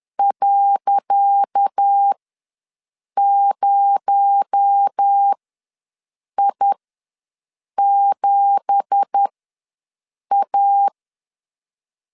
Morse code, heard as a keyed tone, about 10 wpm, spaced as for 6 wpm, 800 Hz. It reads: .0I7A
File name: LEZIONI / 10 SIGNS / 10C DOT ws 10wpm 4213